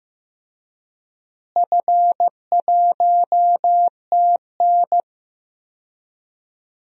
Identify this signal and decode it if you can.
Morse code, heard as a keyed tone, about 15 wpm, 700 Hz. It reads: F1TN
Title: DXCC-100 Call Signs - 4 Characters - Repeated 2x - 15wpm